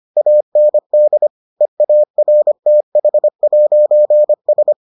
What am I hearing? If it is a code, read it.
Morse code, heard as a keyed tone, about 25 wpm, 600 Hz. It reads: AND EARTH'S